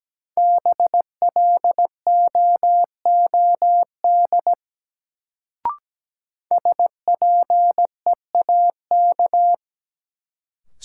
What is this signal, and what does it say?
Morse code, heard as a keyed tone, about 17 wpm, 700 Hz. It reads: BLOOD SPEAK
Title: Sentences from Top 1100 Words - New Word Drill Alt - 17wpm